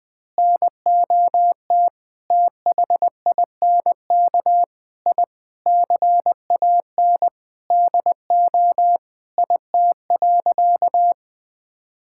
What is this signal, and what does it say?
Morse code, heard as a keyed tone, about 20 wpm, 700 Hz. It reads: NOT THINK I CAN DO IT.
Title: Sentences with 5-Word Phrases - 20wpm